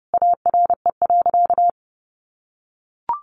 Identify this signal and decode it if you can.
Morse code, heard as a keyed tone, about 29 wpm, 700 Hz. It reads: ARE.
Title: Sentences from Top 500 Words 30wpm